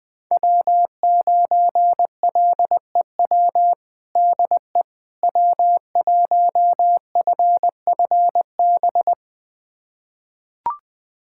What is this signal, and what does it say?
Morse code, heard as a keyed tone, about 20 wpm, 700 Hz. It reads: W9LEW DE W1FFB
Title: USA Callsign DE Callsign - 5 Characters - 20wpm